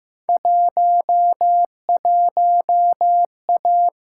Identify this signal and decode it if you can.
Morse code, heard as a keyed tone, about 15 wpm, 700 Hz. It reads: 11A
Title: Field Day Exchanges Encore - 15wpm